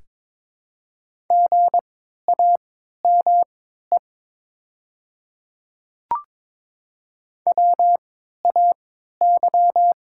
Morse code, heard as a keyed tone, about 22 wpm, spaced as for 13 wpm, 700 Hz. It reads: GAME WAY
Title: 3x Character Spacing - Top 500 Words - 22wpm